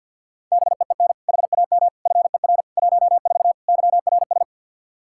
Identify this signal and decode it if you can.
Morse code, heard as a keyed tone, about 50 wpm, 700 Hz. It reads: BEEN HAM FER 14 YRS
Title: Practice QSO - Exchange 3 - More Info - 50wpm